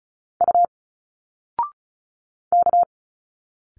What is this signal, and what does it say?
Morse code, heard as a keyed tone, about 36 wpm, 700 Hz. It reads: U K
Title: Single Letters 35wpm